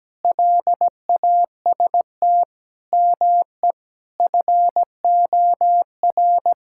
Morse code, heard as a keyed tone, about 17 wpm, 700 Hz. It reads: LAST ME FOR
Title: Sentences from Top 1000 Words - Review Part 6 - 17wpm